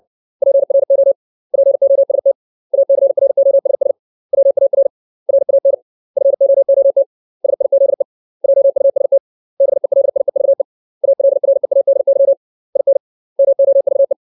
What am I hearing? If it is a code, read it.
Morse code, heard as a keyed tone, about 60 wpm, 550 Hz. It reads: PRO POST ACROSS WAR RAN FOOT SIZE JUST BELIEVE ALREADY IN MOVE
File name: 632 words / 632 Words 60WPM